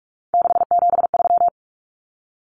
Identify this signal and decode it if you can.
Morse code, heard as a keyed tone, about 44 wpm, 700 Hz. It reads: 673
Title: Common Cut Numbers - 3-Digit 45wpm